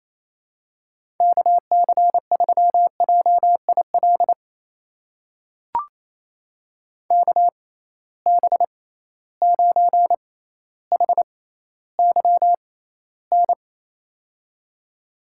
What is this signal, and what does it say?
Morse code, heard as a keyed tone, about 28 wpm, spaced as for 14 wpm, 700 Hz. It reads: KC3JIL KB9HYN
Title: USA Call Signs - 6 Characters - 6x Character Spacing - 28wpm